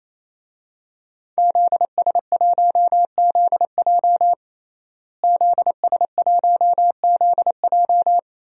Morse code, heard as a keyed tone, about 28 wpm, 700 Hz. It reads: ZS1ZJ ZS1ZJ
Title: DXCC-100 Call Signs - 5 Characters - Repeated 3x - 28wpm